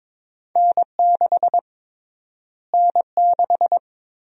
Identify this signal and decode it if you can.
Morse code, heard as a keyed tone, about 22 wpm, 700 Hz. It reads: N6 N6
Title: USA Call Sign Prefixes - Repeated 2x - 22wpm